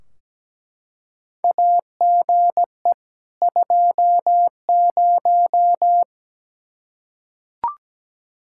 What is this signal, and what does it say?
Morse code, heard as a keyed tone, about 17 wpm, 700 Hz. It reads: AGE 20
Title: Practice QSO - Exchange 3 - More Info - 17wpm